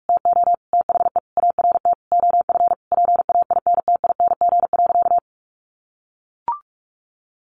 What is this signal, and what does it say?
Morse code, heard as a keyed tone, about 44 wpm, 700 Hz. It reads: TO THE ART OF PAINTING.
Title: Sentences from Top 1200 Words 45wpm